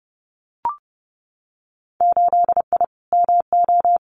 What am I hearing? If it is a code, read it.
Morse code, heard as a keyed tone, about 30 wpm, 700 Hz. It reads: 8I MO
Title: Winter Field Day Exchanges Encore 30wpm